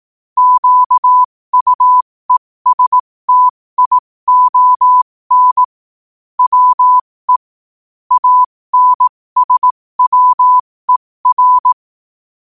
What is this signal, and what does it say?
Morse code, heard as a keyed tone, about 18 wpm, spaced as for 15 wpm, 1000 Hz. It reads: QUESTION WE ANSWER
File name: qotdfiles / 15wpm